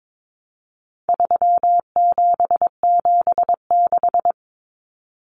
Morse code, heard as a keyed tone, about 22 wpm, 700 Hz. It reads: 3776
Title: Numbers - 4-Digit 22wpm